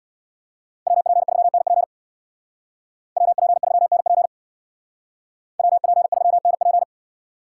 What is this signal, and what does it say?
Morse code, heard as a keyed tone, about 58 wpm, 700 Hz. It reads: WP3NP WP3NP WP3NP
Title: Speed Racing - USA 2x2 Call Signs 50wpm